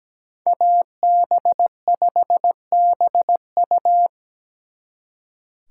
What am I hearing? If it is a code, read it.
Morse code, heard as a keyed tone, about 20 wpm, 700 Hz. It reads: AB5BU
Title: Speed Racing - USA 2x2 Call Signs 17wpm